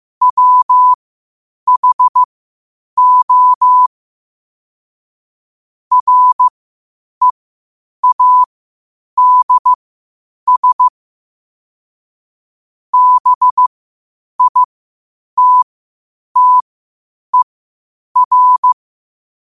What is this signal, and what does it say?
Morse code, heard as a keyed tone, about 15 wpm, spaced as for 8 wpm, 1000 Hz. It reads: WHO READS BITTER
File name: qotdfiles / 8wpm